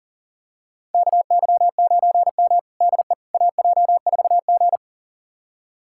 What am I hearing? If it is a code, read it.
Morse code, heard as a keyed tone, about 40 wpm, 700 Hz. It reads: KY9M DE AJ4G
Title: USA Callsign DE Callsign - 4 Characters - 40wpm